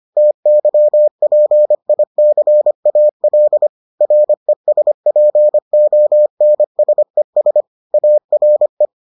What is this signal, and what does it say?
Morse code, heard as a keyed tone, about 25 wpm, 600 Hz. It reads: TYPICAL RESPONSES ARE